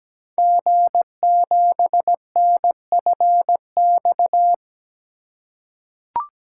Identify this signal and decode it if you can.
Morse code, heard as a keyed tone, about 17 wpm, 700 Hz. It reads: G7NFX
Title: Code Groups - 5 Letter:Numbers - 17wpm